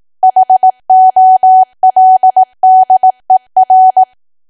Morse code, heard as a keyed tone, about 18 wpm, 750 Hz. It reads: HOLDER
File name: cwa / W1AW-Level2 / 18WPM